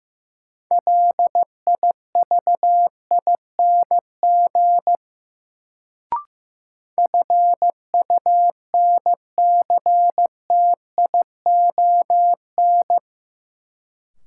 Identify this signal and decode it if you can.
Morse code, heard as a keyed tone, about 16 wpm, 700 Hz. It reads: LIVING FUNCTION
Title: Sentences from Top 1100 Words - New Word Drill Alt - 15wpm